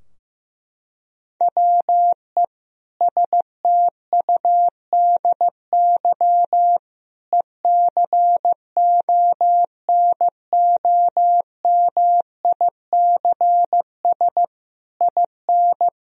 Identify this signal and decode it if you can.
Morse code, heard as a keyed tone, about 16 wpm, 700 Hz. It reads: WE STUDY ECONOMICS IN
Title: Sentences from Top 1000 Words - Review Part 3 - 15wpm